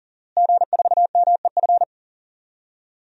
Morse code, heard as a keyed tone, about 40 wpm, 700 Hz. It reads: G4MEF
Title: DXCC-100 Call Signs - 5 Characters - Repeated 2x - 40wpm